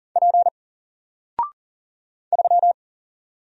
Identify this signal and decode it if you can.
Morse code, heard as a keyed tone, about 40 wpm, 700 Hz. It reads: P 3
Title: Single Letter-Number Alt 40wpm